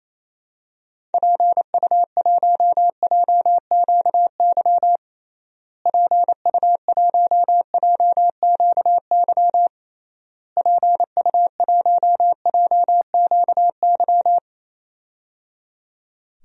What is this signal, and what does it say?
Morse code, heard as a keyed tone, about 28 wpm, 700 Hz. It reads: PU1JQY PU1JQY PU1JQY
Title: DXCC-100 Call Signs - 6 Characters - Repeated 3x - 28wpm